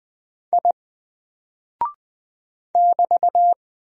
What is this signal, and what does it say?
Morse code, heard as a keyed tone, about 20 wpm, 700 Hz. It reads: I <BT>
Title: Single Character (Letter:Number:Punctuation:Prosign) 20wpm